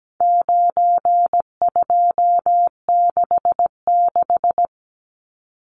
Morse code, heard as a keyed tone, about 17 wpm, 700 Hz. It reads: 9266
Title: Numbers - 4-Digit 17wpm